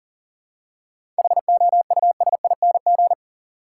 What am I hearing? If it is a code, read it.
Morse code, heard as a keyed tone, about 40 wpm, 700 Hz. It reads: HOUSING
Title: Seven Letter Words - 40wpm